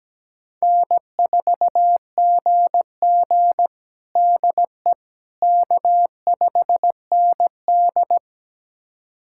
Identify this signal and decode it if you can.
Morse code, heard as a keyed tone, about 17 wpm, 700 Hz. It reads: N4GG DE K5ND
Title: USA Callsign DE Callsign - 4 Characters - 17wpm